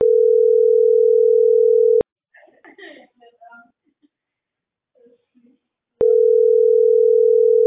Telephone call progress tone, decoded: ringback tone

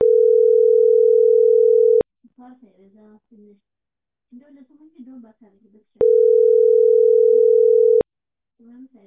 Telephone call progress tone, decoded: ringback tone

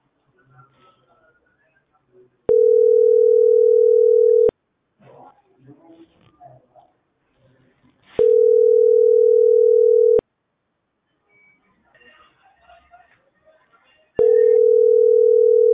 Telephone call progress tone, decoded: ringback tone

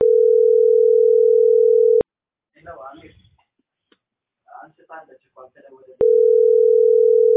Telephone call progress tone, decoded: ringback tone

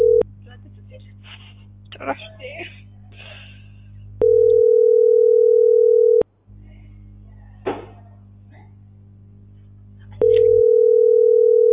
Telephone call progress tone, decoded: ringback tone

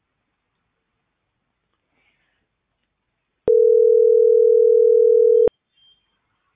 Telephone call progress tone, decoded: ringback tone

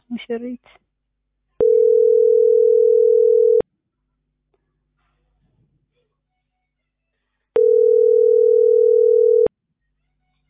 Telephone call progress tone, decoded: ringback tone

